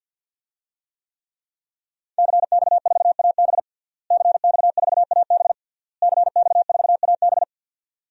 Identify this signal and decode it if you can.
Morse code, heard as a keyed tone, about 48 wpm, 700 Hz. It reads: CX4AB CX4AB CX4AB
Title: DXCC-100 Call Signs - 5 Characters - Repeated 3x - 50wpm